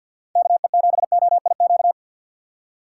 Morse code, heard as a keyed tone, about 50 wpm, 700 Hz. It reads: KE7OIQ
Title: USA Call Signs - 6 Characters - 3x Character Spacing - 50wpm